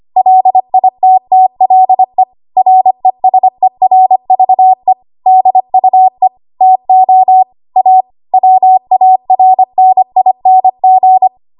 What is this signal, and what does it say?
Morse code, heard as a keyed tone, about 25 wpm, 750 Hz. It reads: LITTLE RESERVE DUE TO A WARNING
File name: Archive / 25 WPM / 25WPM